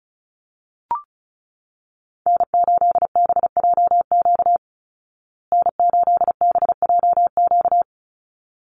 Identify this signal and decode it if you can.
Morse code, heard as a keyed tone, about 34 wpm, 700 Hz. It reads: N8BJQ N8BJQ
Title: Call Signs Repeated 3x - 35wpm